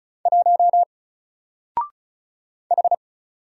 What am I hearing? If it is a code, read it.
Morse code, heard as a keyed tone, about 34 wpm, 700 Hz. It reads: 1 H
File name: Single-Letter-Number-35wpm